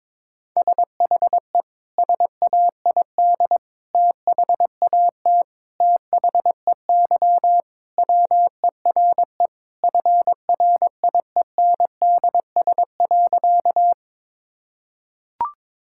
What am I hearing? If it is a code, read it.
Morse code, heard as a keyed tone, about 22 wpm, 700 Hz. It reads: SHE SAID THAT THEY WERE FRIENDS.
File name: practice / 22wpm / Sentences with 3-Word Phrases - 22wpm